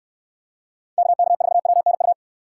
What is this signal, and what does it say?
Morse code, heard as a keyed tone, about 66 wpm, 700 Hz. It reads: KK4WNU